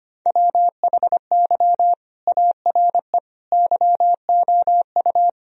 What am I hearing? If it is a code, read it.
Morse code, heard as a keyed tone, about 25 wpm, 700 Hz. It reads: WHY ARE YOU
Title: Sentences from Top 1000 Words - Review Part 3 - 25wpm